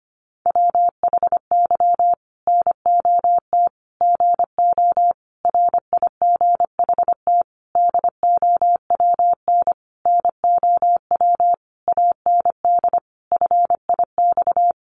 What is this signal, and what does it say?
Morse code, heard as a keyed tone, about 25 wpm, 700 Hz. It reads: WHY NOT GO RIGHT DOWN NOW AND FIX
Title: Sentences from Top 1400 Words 25wpm